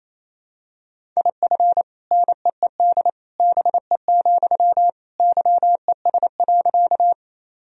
Morse code, heard as a keyed tone, about 28 wpm, 700 Hz. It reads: IF NEED BE, YES.
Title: Sentences from Top 1000 Words - Review Part 1 - 28wpm